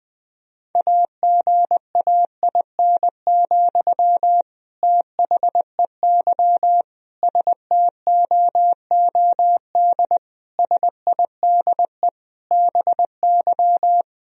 Morse code, heard as a keyed tone, about 20 wpm, 700 Hz. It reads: AGAIN, THEY STOOD SIDE BY
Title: Sentences from Top 1000 Words - Review Part 2 - 20wpm